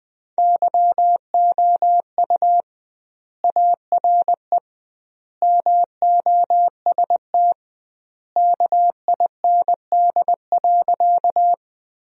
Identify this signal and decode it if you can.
Morse code, heard as a keyed tone, about 20 wpm, 700 Hz. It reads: YOU ARE MOST KIND.
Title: 2x Word-Spacing - Sentences from Top 200 Words - 20wpm